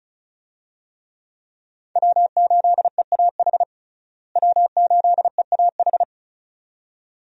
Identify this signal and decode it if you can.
Morse code, heard as a keyed tone, about 36 wpm, 700 Hz. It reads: W8EAH W8EAH
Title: USA Call Signs - 5 Characters - Repeated 2x - 35wpm